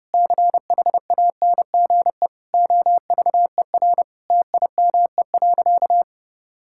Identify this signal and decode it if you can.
Morse code, heard as a keyed tone, about 30 wpm, 700 Hz. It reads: CHANGE OVER TIME.
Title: Sentences from Top 1000 Words - Review Part 6 - 30wpm